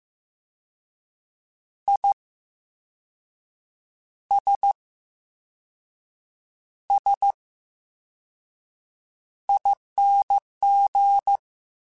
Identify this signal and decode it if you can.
Morse code, heard as a keyed tone, about 15 wpm, spaced as for 5 wpm, 800 Hz. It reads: ISSING
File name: Special INGs / Special INGs-mp3s / Slow ing @13